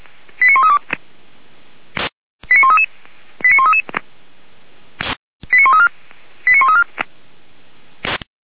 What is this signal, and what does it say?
Five-tone selective call (ZVEI), heard as quick runs of five tones, tones of about 70 ms. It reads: 89132 89133 89133 89134 89134